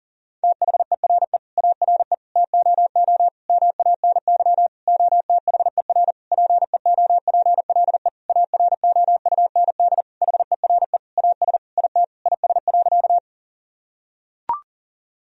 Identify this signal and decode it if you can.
Morse code, heard as a keyed tone, about 40 wpm, 700 Hz. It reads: THERE ARE TOO MANY OTHER PEOPLE AROUND HERE AS IT IS.
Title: Sentences from Top 1000 Words - Review Part 3 - 40wpm